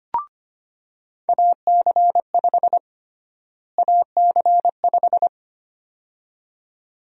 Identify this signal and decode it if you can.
Morse code, heard as a keyed tone, about 25 wpm, 700 Hz. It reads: AC5 AC5
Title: USA Call Sign Prefixes - Repeated 2x - 25wpm